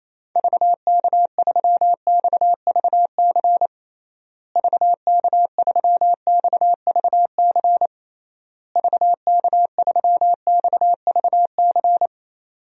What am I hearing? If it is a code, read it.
Morse code, heard as a keyed tone, about 28 wpm, 700 Hz. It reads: VK3XVC VK3XVC VK3XVC
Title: DXCC-100 Call Signs - 6 Characters - Repeated 3x - 28wpm